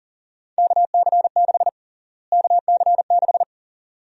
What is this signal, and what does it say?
Morse code, heard as a keyed tone, about 40 wpm, 700 Hz. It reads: KC6 KC6
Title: USA Call Sign Prefixes - Repeated 2x - 40wpm